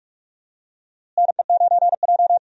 Morse code, heard as a keyed tone, about 45 wpm, 700 Hz. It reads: NE9J
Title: USA Call Signs - 4 Characters - Repeated 3x - 45wpm